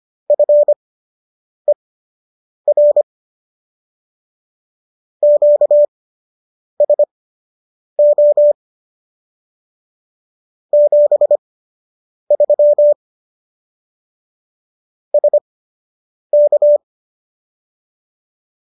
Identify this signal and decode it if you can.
Morse code, heard as a keyed tone, about 25 wpm, spaced as for 8 wpm, 590 Hz. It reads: FER QSO 73 SK
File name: qso003 08